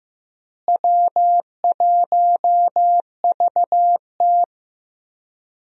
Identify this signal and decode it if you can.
Morse code, heard as a keyed tone, about 15 wpm, 700 Hz. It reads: W1VT
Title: USA Call Signs - 4 Characters - Repeated 3x - 15wpm